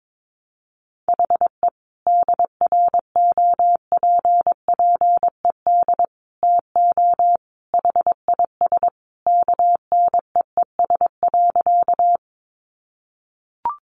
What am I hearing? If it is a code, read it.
Morse code, heard as a keyed tone, about 22 wpm, 700 Hz. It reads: HE DROPPED TO HIS KNEES.
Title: A Sound Of Thunder 22wpm